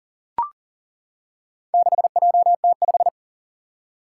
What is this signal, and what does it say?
Morse code, heard as a keyed tone, about 41 wpm, 700 Hz. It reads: 6JT5
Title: Four Challenging Letter-Number Combinations 40wpm